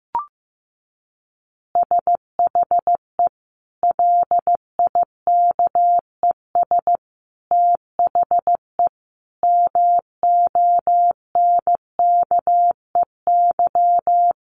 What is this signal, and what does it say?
Morse code, heard as a keyed tone, about 15 wpm, 700 Hz. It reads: SHE LIKES THE MONKEY
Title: Easy Phrases 15wpm